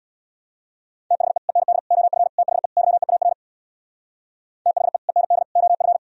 Morse code, heard as a keyed tone, about 64 wpm, 700 Hz. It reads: THE END OF THE YEAR THE END OF